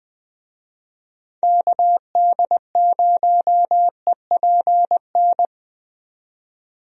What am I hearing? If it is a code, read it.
Morse code, heard as a keyed tone, about 20 wpm, 700 Hz. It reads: KD0EPN
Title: USA Call Signs - 6 Characters - 20wpm